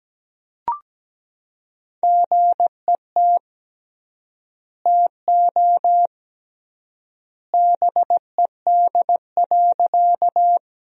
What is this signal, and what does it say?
Morse code, heard as a keyed tone, about 17 wpm, 700 Hz. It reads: GET TO BED.
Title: 3x Word-Spacing - Sentences from Top 500 Words - 17wpm